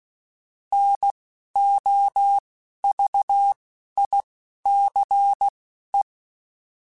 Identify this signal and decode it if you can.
Morse code, heard as a keyed tone, about 16 wpm, spaced as for 11 wpm, 800 Hz. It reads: NOVICE
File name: fromSMs / FromSMs mp3s / Classes@10